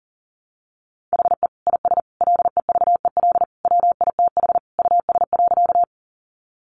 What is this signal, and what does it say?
Morse code, heard as a keyed tone, about 40 wpm, 700 Hz. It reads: HE IS LEVEL WITH US.
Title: Sentences from Top 700 Words 40wpm